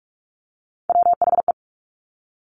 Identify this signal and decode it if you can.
Morse code, heard as a keyed tone, about 46 wpm, 700 Hz. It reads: WHE
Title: Top Three Letter Combinations in Words 45wpm